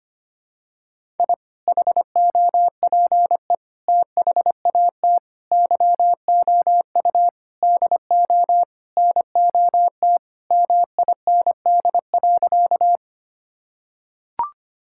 Morse code, heard as a keyed tone, about 25 wpm, 700 Hz. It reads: I HOPE THAT YOU DO NOT MIND.
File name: practice / 25wpm / Sentences with 4-Word Phrases - 25wpm